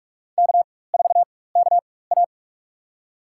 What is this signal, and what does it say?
Morse code, heard as a keyed tone, about 44 wpm, spaced as for 21 wpm, 700 Hz. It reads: K4KA
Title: USA Call Signs - 4 Characters - 4x Character Spacing - 45wpm